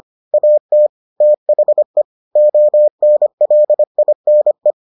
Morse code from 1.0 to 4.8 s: THE ONLINE

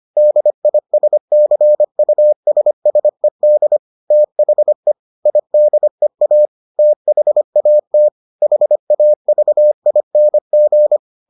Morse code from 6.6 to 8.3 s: THAT

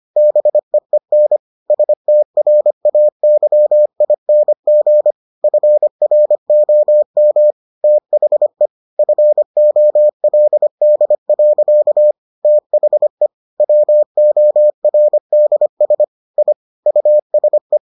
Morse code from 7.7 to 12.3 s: THE FOLD.

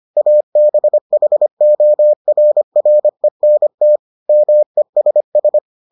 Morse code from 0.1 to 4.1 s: ABHORRENT

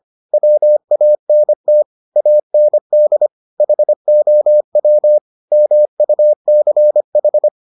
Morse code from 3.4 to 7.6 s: HOW MUCH

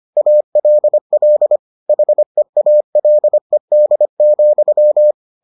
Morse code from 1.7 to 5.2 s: HEALED,

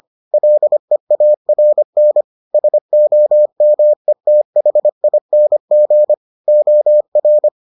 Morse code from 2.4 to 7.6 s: SOMETHING OR

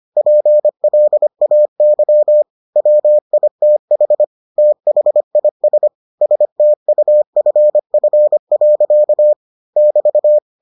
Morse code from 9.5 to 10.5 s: <BT>

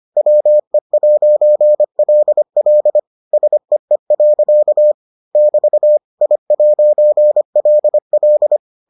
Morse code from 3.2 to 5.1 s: SEE.

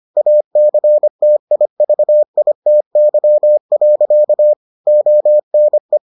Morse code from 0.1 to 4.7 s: ACTIVITY.

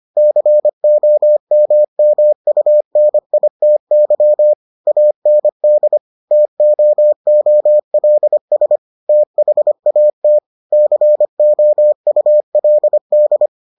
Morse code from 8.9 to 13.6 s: THAT COULD

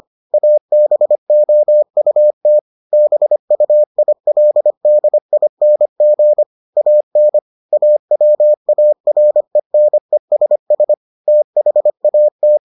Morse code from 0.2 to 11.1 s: ABOUT BUILDING AN AWARENESS

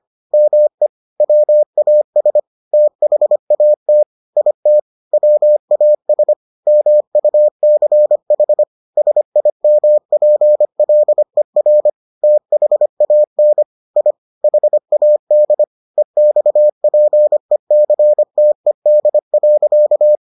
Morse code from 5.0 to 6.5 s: WAS

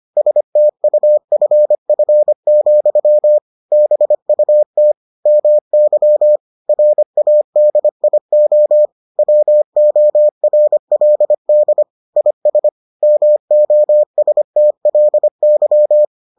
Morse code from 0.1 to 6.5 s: STUFF, BUT MY

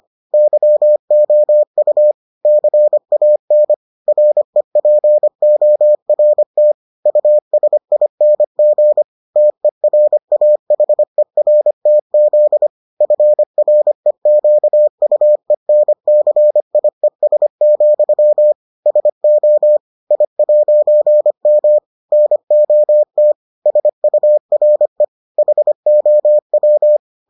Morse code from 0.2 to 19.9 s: YOU CAN REPORT USING TERAHERTZ FREQUENCIES, SO

